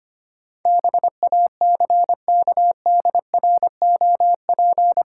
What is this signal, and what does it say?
Morse code, heard as a keyed tone, about 25 wpm, 700 Hz. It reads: BACKDROP
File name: Eight Letter Words - 25wpm